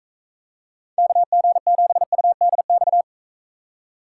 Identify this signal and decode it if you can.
Morse code, heard as a keyed tone, about 42 wpm, 700 Hz. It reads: KG7UDX